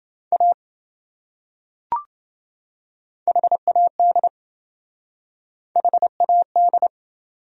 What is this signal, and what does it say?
Morse code, heard as a keyed tone, about 27 wpm, 700 Hz. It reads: A HAD HAD